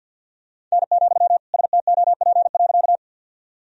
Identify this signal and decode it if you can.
Morse code, heard as a keyed tone, about 52 wpm, 700 Hz. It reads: N, STOP.
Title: Sentences from Top 300 Words and Abbreviations 50wpm